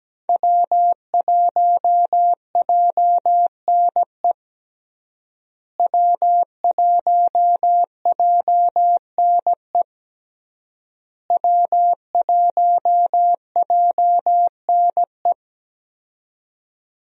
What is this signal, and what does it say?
Morse code, heard as a keyed tone, about 17 wpm, 700 Hz. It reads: W1JNE W1JNE W1JNE
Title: USA Call Signs - 5 Characters - Repeated 3x - 17wpm